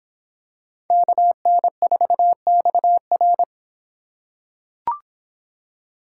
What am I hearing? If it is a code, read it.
Morse code, heard as a keyed tone, about 24 wpm, 700 Hz. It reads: KN4XR